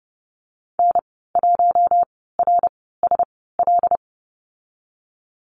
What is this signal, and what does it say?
Morse code, heard as a keyed tone, about 30 wpm, spaced as for 17 wpm, 700 Hz. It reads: N1RSL
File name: Lesson 14g - Number 2 - Call Signs 30wpm